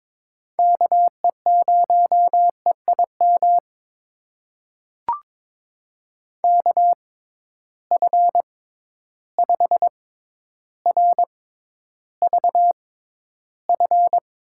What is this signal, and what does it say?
Morse code, heard as a keyed tone, about 22 wpm, spaced as for 11 wpm, 700 Hz. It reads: KE0EIM KF5RVF
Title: USA Call Signs - 6 Characters - 6x Character Spacing - 22wpm